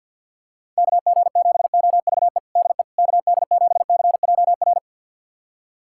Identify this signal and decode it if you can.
Morse code, heard as a keyed tone, about 51 wpm, 700 Hz. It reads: KG7OVE DE KD7CJR
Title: USA Callsign DE Callsign 50wpm